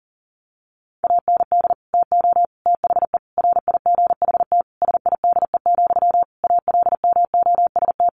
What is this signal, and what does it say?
Morse code, heard as a keyed tone, about 40 wpm, 700 Hz. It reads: AND TO THE RIGHT SIDE, ALMOST
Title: Sentences from Top 1300 Words 40wpm